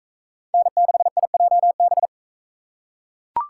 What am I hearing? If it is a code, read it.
Morse code, heard as a keyed tone, about 40 wpm, 700 Hz. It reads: N6IJB